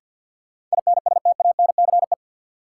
Morse code, heard as a keyed tone, about 46 wpm, 700 Hz. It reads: INSTANCE
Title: Sentences from Top 800 Words - New Word Drill - 50wpm